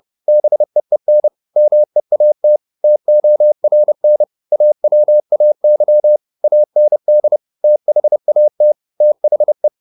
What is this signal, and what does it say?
Morse code, heard as a keyed tone, about 30 wpm, 600 Hz. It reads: BEEN MEAT TORN AWAY AND THAT THE